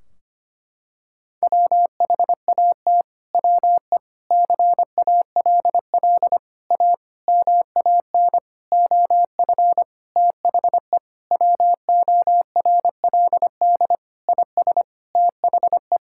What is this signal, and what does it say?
Morse code, heard as a keyed tone, about 25 wpm, 700 Hz. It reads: WHAT WE CALL A MAN OF THE WORLD IS THE
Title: Sentences from Top 1600 Words 25wpm